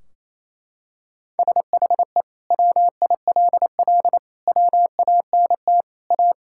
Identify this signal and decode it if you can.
Morse code, heard as a keyed tone, about 28 wpm, 700 Hz. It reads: SHE WILL WANT A
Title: Sentences from Top 1600 Words 28wpm